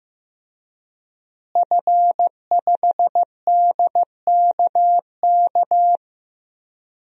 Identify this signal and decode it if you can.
Morse code, heard as a keyed tone, about 15 wpm, 700 Hz. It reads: F5DKK
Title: DXCC-100 Call Signs - 5 Characters - Repeated 3x - 15wpm